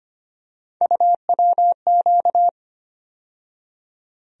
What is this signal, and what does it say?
Morse code, heard as a keyed tone, about 25 wpm, 700 Hz. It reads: UWQ
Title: USA Call Sign Postfixes - 3 Letters 25wpm